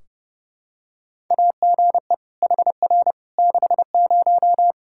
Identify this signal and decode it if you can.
Morse code, heard as a keyed tone, about 32 wpm, 700 Hz. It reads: AGE HR 60
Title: Practice QSO - Exchange 3 - More Info - 30wpm